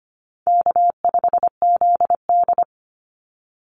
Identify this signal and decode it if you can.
Morse code, heard as a keyed tone, about 25 wpm, 700 Hz. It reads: K5ZD
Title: Easy Call Signs - 25wpm